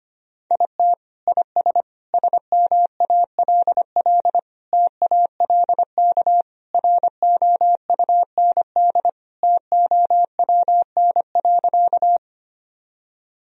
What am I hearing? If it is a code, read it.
Morse code, heard as a keyed tone, about 25 wpm, 700 Hz. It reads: IT IS SMALL TALK ROUND TOWN.
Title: Sentences from Top 1000 Words - Review Part 1 - 25wpm